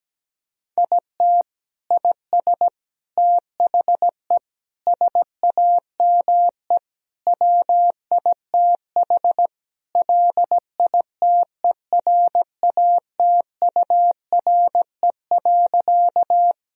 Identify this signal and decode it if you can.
Morse code, heard as a keyed tone, about 17 wpm, 700 Hz. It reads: IT IS THE SAME WITH LITERATURE.
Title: Sentences from Top 1000 Words - Review Part 3 - 17wpm